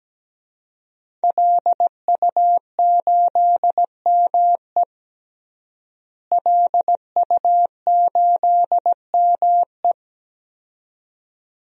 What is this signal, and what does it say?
Morse code, heard as a keyed tone, about 17 wpm, 700 Hz. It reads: LU8ME LU8ME
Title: DXCC-100 Call Signs - 5 Characters - Repeated 2x - 17wpm